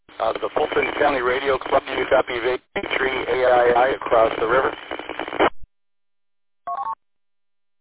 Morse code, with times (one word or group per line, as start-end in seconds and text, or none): none